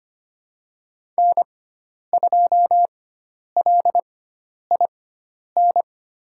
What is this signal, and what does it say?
Morse code, heard as a keyed tone, about 25 wpm, spaced as for 10 wpm, 700 Hz. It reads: N2LIN
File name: Lesson 16g - Letter c - Call Signs 25wpm